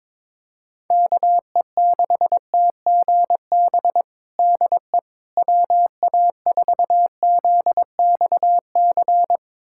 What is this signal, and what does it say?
Morse code, heard as a keyed tone, about 22 wpm, 700 Hz. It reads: KE6TGB DE WA4ZXC
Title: USA Callsign DE Callsign 22wpm